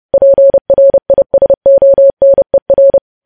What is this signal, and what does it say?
Morse code, heard as a keyed tone, about 30 wpm, 550 Hz. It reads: PRISONER
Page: news headlines in morse code at 30 wpm